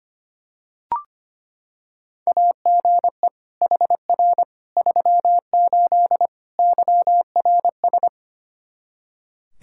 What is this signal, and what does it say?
Morse code, heard as a keyed tone, about 25 wpm, 700 Hz. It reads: AGE HR 38 YRS
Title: Practice QSO - Exchange 3 - More Info - 25wpm